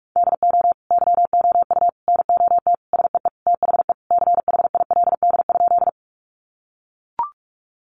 Morse code, heard as a keyed tone, about 44 wpm, 700 Hz. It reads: DO YOU NOT SEE THE CHILD?
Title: Sentences from Top 800 Words 45wpm